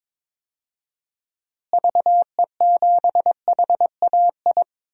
Morse code, heard as a keyed tone, about 22 wpm, 700 Hz. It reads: VE7HAI
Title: DXCC-100 Call Signs - 6 Characters - Repeated 2x - 22wpm